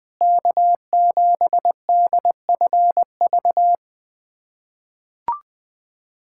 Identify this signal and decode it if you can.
Morse code, heard as a keyed tone, about 20 wpm, 700 Hz. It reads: K7DFV
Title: USA Callsign DE Callsign - 5 Characters - 20wpm